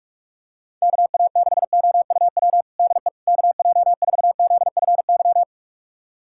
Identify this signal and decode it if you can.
Morse code, heard as a keyed tone, about 42 wpm, 700 Hz. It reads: KA6OUW DE KJ4ZFY
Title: USA Callsign DE Callsign - 6 Characters - 45wpm